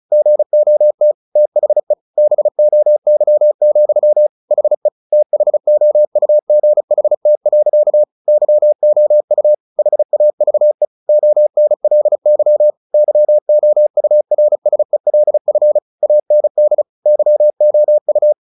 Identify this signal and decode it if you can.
Morse code, heard as a keyed tone, about 34 wpm, 600 Hz. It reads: GOT THE BOY, HE THOUGHT. YOU HAVE ONLY YOURSELF AND YOU
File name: the old man and the sea / the old man and the sea 1 35wpm